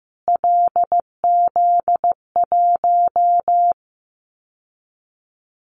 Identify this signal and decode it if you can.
Morse code, heard as a keyed tone, about 15 wpm, 700 Hz. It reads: LZ1
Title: Callsign Prefixes for Top 25 DXCC - Repeated 3x - 15wpm